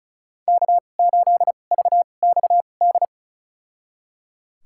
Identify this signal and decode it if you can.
Morse code, heard as a keyed tone, about 34 wpm, spaced as for 25 wpm, 700 Hz. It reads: K8VXD
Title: USA Call Signs - 5 Characters - 2x Character Spacing - 35wpm